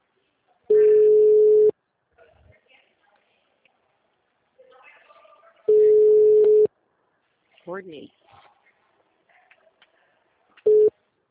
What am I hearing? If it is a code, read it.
Ringback tone (Europe, 425 Hz), with speech.
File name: out- -2026- - - .6212